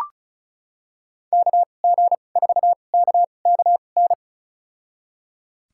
Morse code, heard as a keyed tone, about 34 wpm, spaced as for 25 wpm, 700 Hz. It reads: KG4KKN
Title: USA Call Signs - 6 Characters - 2x Character Spacing - 35wpm